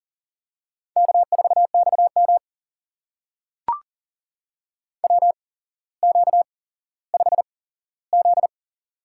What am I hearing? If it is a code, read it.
Morse code, heard as a keyed tone, about 40 wpm, spaced as for 17 wpm, 700 Hz. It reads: K4XM WQ5Z
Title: USA Call Signs - 4 Characters - 8x Character Spacing - 40wpm